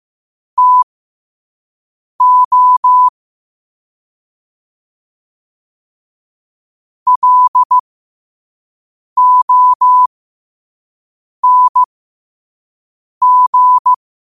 Morse code, heard as a keyed tone, about 15 wpm, spaced as for 5 wpm, 1000 Hz. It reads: TO LONG
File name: qotdfiles / 5wpm